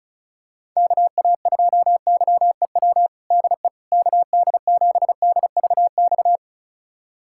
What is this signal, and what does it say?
Morse code, heard as a keyed tone, about 36 wpm, 700 Hz. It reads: KA2YEW DE KD7DVX
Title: USA Callsign DE Callsign - 6 Characters - 35wpm